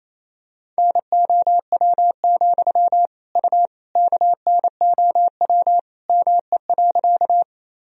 Morse code, heard as a keyed tone, about 28 wpm, 700 Hz. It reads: NOW, U KNOW ME.
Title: Sentences from Top 200 Words and Abbreviations 28wpm